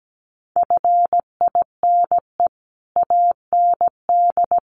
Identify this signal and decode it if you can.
Morse code, heard as a keyed tone, about 17 wpm, 700 Hz. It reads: FINE AND
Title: Sentences from Top 900 Words 17wpm